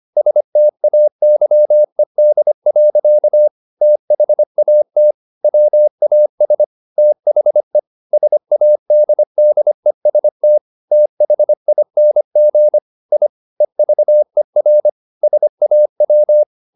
Morse code, heard as a keyed tone, about 25 wpm, 600 Hz. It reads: STAYED. THAT WAS THE SADDEST THING I EVER SAW
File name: the old man and the sea part 1 25wpm